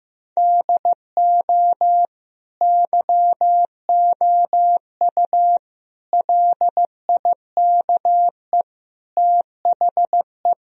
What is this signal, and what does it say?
Morse code, heard as a keyed tone, about 16 wpm, 700 Hz. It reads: DO YOU LIKE THE
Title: Sentences from Top 1000 Words - Review Part 4 - 15wpm